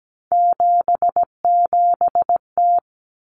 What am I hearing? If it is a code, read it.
Morse code, heard as a keyed tone, about 17 wpm, 700 Hz. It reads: 77T
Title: Common Cut Numbers - 3-Digit 17wpm